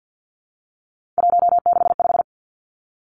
Morse code, heard as a keyed tone, about 50 wpm, 700 Hz. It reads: 165
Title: Three Challenging Letter-Number Combinations 50wpm